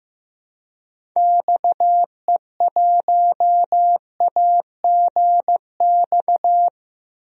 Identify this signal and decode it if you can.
Morse code, heard as a keyed tone, about 15 wpm, 700 Hz. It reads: XE1AGX
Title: DXCC-100 Call Signs - 6 Characters - Repeated 2x - 15wpm